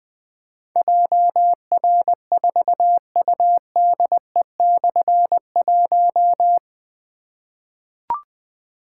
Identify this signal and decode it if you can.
Morse code, heard as a keyed tone, about 20 wpm, 700 Hz. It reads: JR4UDE/1
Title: Difficult Call Signs 20wpm